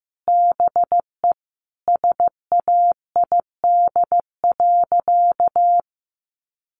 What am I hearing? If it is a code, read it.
Morse code, heard as a keyed tone, about 15 wpm, 700 Hz. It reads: BE SAID.
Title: Sentences from Top 500 Words 15wpm